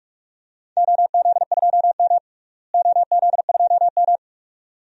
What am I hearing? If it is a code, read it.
Morse code, heard as a keyed tone, about 42 wpm, 700 Hz. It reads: OZ2M OZ2M
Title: DXCC-100 Call Signs - 4 Characters - Repeated 2x - 45wpm